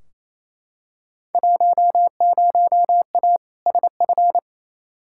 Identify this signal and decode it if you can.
Morse code, heard as a keyed tone, about 28 wpm, 700 Hz. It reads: 10A SF
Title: Field Day Exchanges Encore - 28wpm